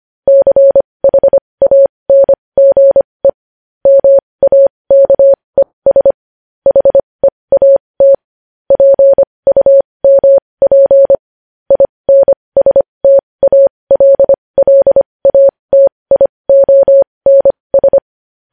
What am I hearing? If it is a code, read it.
Morse code, heard as a keyed tone, about 25 wpm, spaced as for 20 wpm, 550 Hz. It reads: CHANGE MAKES HEAT PUMP INSTALLATIONS
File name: podcast / News headlines-1x-25x20